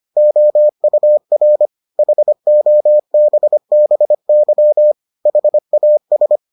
Morse code, heard as a keyed tone, about 25 wpm, 600 Hz. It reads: OUR HOBBY HAS